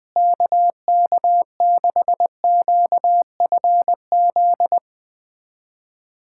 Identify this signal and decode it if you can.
Morse code, heard as a keyed tone, about 20 wpm, 700 Hz. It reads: KK6QFZ
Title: USA Callsign DE Callsign 20wpm